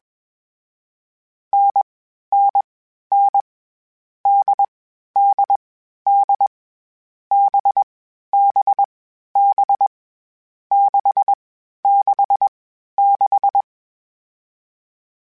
Morse code, heard as a keyed tone, about 21 wpm, 800 Hz. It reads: N N N D D D B B B 6 6 6
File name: Variable Speed / Variable Speed-mp3s / GZ GZ7 sp @20